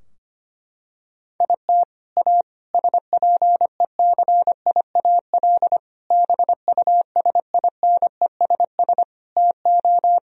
Morse code, heard as a keyed tone, about 25 wpm, 700 Hz. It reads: IT A SPECIAL BUSINESS TO
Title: Sentences from Top 1000 Words - Review Part 3 - 25wpm